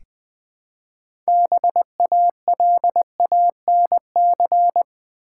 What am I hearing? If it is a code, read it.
Morse code, heard as a keyed tone, about 20 wpm, 700 Hz. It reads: BALANC
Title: Build A Word - 2 Letters At A Time - 8-Letter Words - 20wpm